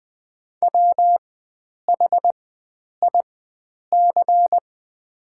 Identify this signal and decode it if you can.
Morse code, heard as a keyed tone, about 20 wpm, spaced as for 9 wpm, 690 Hz. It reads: WHIC
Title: 4x Character Spacing - Top 300 Words - 20wpm